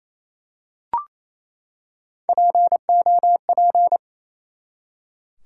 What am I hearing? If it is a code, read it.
Morse code, heard as a keyed tone, about 28 wpm, 700 Hz. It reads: POP
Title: Three Letter Words - 28wpm